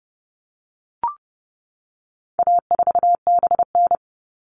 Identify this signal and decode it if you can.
Morse code, heard as a keyed tone, about 30 wpm, 700 Hz. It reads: A4BN
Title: All Cut Numbers - 4-Digit 30wpm